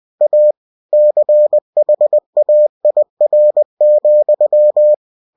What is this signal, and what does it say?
Morse code, heard as a keyed tone, about 20 wpm, 600 Hz. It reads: A CHAIR,
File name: cw-books / winnie-the-pooh / winnie-20wpm-page-0037